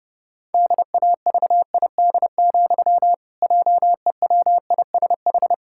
Morse code, heard as a keyed tone, about 30 wpm, 700 Hz. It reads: DAVID, JEWISH
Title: Pangram Challenge v2 - 30wpm